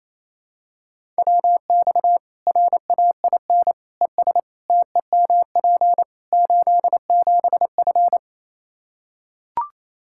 Morse code, heard as a keyed tone, about 28 wpm, 700 Hz. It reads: WX RAIN ES TEMP 87F
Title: Practice QSO - Exchange 2 - Info - 28wpm